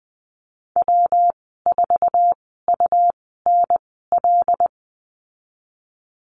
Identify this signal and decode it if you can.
Morse code, heard as a keyed tone, about 20 wpm, spaced as for 15 wpm, 700 Hz. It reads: W4UNL
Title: 238 Lesson 38g - Letter q - Call Signs 20wpm